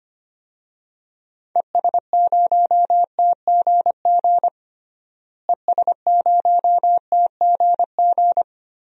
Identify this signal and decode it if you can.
Morse code, heard as a keyed tone, about 25 wpm, 700 Hz. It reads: ES0TGG ES0TGG
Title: DXCC-100 Call Signs - 6 Characters - Repeated 3x - 25wpm